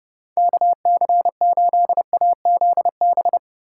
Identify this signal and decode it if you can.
Morse code, heard as a keyed tone, about 30 wpm, 700 Hz. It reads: KC8AZB
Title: USA Call Signs - 6 Characters - 2x Character Spacing - 30wpm